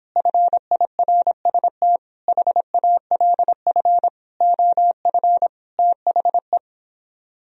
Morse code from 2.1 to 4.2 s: HALF

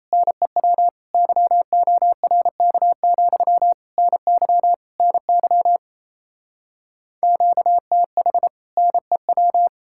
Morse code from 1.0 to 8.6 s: YORK, NY NY QTH

